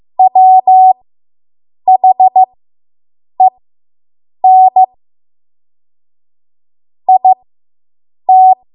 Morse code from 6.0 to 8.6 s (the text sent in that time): IT